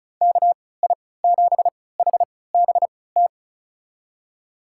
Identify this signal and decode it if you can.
Morse code, heard as a keyed tone, about 36 wpm, spaced as for 20 wpm, 700 Hz. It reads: KI7HBT
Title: USA Call Signs - 6 Characters - 3x Character Spacing - 35wpm